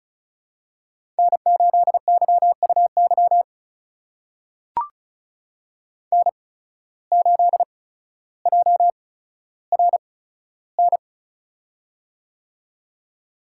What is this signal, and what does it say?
Morse code, heard as a keyed tone, about 36 wpm, spaced as for 15 wpm, 700 Hz. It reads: N8YUY N8JRN
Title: USA Call Signs - 8x Character Spacing - 35wpm